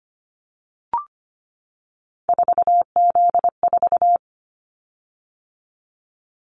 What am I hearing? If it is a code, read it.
Morse code, heard as a keyed tone, about 25 wpm, 700 Hz. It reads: 4Z4
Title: Callsign Prefixes for Top 50 DXCC - 25wpm